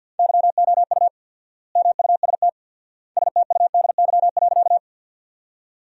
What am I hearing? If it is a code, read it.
Morse code, heard as a keyed tone, about 50 wpm, 700 Hz. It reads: YOU MUST STUDY.
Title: 4x Word-Spacing - Sentences from Top 300 Words - 50wpm